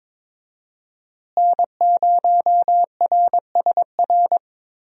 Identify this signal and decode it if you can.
Morse code, heard as a keyed tone, about 22 wpm, 700 Hz. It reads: N0RSR
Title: USA Call Signs - 5 Characters - Repeated 2x - 22wpm